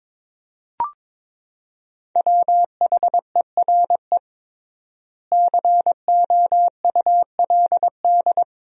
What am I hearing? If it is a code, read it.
Morse code, heard as a keyed tone, about 22 wpm, 700 Hz. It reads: WHERE COULD
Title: 3x Word-Spacing - Sentences from Top 200 Words - 22wpm